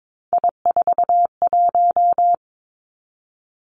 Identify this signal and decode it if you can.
Morse code, heard as a keyed tone, about 22 wpm, 700 Hz. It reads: I41
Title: Three Challenging Letter-Number Combinations 22wpm